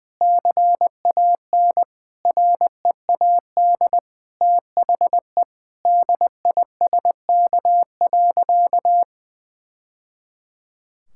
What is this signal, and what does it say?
Morse code, heard as a keyed tone, about 19 wpm, 700 Hz. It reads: CAN READ THE DISK.
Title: Sentences from Top 1000 Words - Review Part 6 - 20wpm